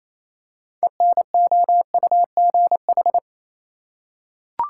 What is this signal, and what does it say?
Morse code, heard as a keyed tone, about 28 wpm, 700 Hz. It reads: ENOUGH
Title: Six Letter Words - 28wpm